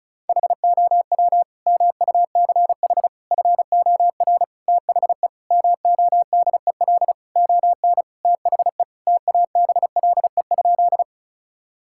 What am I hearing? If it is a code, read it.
Morse code, heard as a keyed tone, about 34 wpm, 700 Hz. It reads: HOW MUCH FOR THE MODEL ON THE TABLE?
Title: Sentences from Top 1000 Words - Review Part 5 - 35wpm